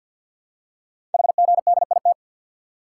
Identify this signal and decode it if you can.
Morse code, heard as a keyed tone, about 48 wpm, 700 Hz. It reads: HGDIT
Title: Code Groups - 5 Letters - 50wpm